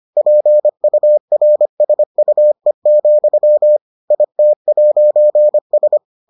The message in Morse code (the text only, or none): PURSUE, IT'S